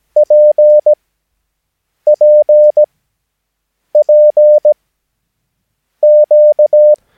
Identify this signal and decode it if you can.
Morse code, heard as a keyed tone, about 17 wpm, spaced as for 6 wpm, 600 Hz. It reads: PPPQ
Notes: Övningsmaterial med alfabetet A till Ö samt siffror 0 till 9 enligt International/Continental Morse Code.